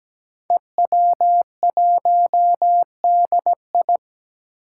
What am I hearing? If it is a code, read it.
Morse code, heard as a keyed tone, about 17 wpm, 700 Hz. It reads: EW1DI
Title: Five Challenging Letter-Number Combinations 17wpm